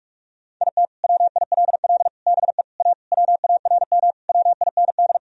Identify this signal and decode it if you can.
Morse code, heard as a keyed tone, about 41 wpm, 700 Hz. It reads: IT WILL BE A WARM WIND
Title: Sentences from Top 1000 Words - Review Part 3 - 45wpm